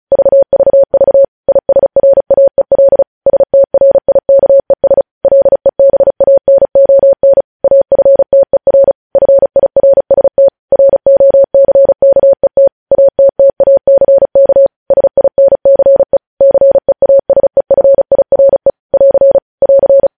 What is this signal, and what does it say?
Morse code, heard as a keyed tone, about 36 wpm, 550 Hz. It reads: VVV ISRAEL STRIKES LEBANON AFTER FIRST ROCKET ATTACK SINCE CEASEFIRE <AR> <AR>
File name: podcast / News headlines-1x-35x35